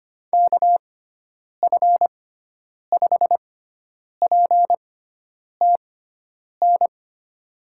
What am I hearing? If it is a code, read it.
Morse code, heard as a keyed tone, about 25 wpm, spaced as for 9 wpm, 700 Hz. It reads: KF5PTN
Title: USA Call Signs - 6x Character Spacing - 25wpm